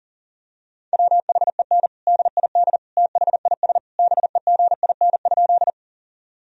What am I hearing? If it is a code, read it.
Morse code, heard as a keyed tone, about 40 wpm, 700 Hz. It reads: WHEN DID THIS BEGIN?
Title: Sentences from Top 1000 Words - Review Part 1 - 40wpm